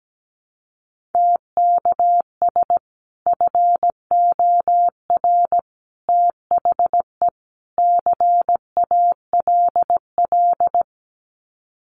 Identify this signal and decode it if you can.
Morse code, heard as a keyed tone, about 17 wpm, 700 Hz. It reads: TKS FOR THE CALL
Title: Common QSO Phrases 17wpm